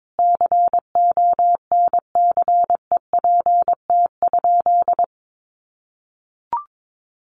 Morse code, heard as a keyed tone, about 22 wpm, 700 Hz. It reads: CONCEPT?
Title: Sentences from Top 1100 Words 22wpm